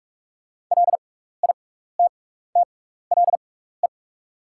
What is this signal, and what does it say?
Morse code, heard as a keyed tone, about 43 wpm, spaced as for 15 wpm, 690 Hz. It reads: LITTLE
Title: 6x Character Spacing - Top 200 Words - 45wpm